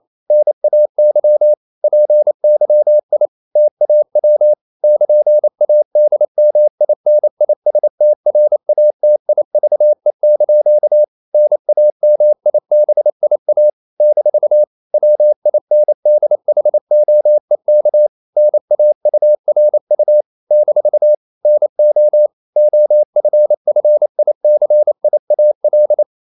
Morse code, heard as a keyed tone, about 28 wpm, 600 Hz. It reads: NAY PYI TAW <KN>ADMINISTRATIVE) NAMIBIA - WINDHOEK NAURU - NO OFFICIAL
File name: CW / Practice / NationalCapitals28